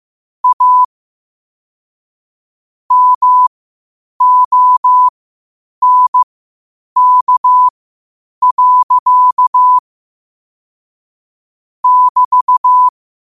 Morse code, heard as a keyed tone, about 15 wpm, spaced as for 8 wpm, 1000 Hz. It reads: A MONK. <BT>
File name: qotdfiles / 8wpm